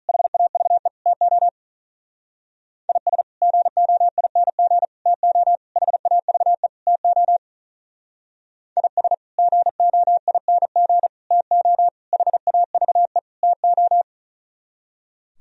Morse code, heard as a keyed tone, about 43 wpm, 700 Hz. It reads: HAVE TO IS GOING TO HAVE TO IS GOING TO HAVE TO
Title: Speed-Racing - Sets of 5 Words 35wpm